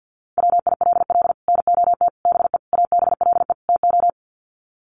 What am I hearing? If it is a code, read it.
Morse code, heard as a keyed tone, about 50 wpm, 700 Hz. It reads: WILL NOT BE ABLE TO
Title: Sets of 5 Words 50wpm